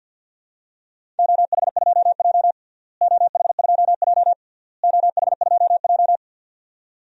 Morse code, heard as a keyed tone, about 52 wpm, 700 Hz. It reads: OH2J OH2J OH2J
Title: DXCC-100 Call Signs - 4 Characters - Repeated 3x - 50wpm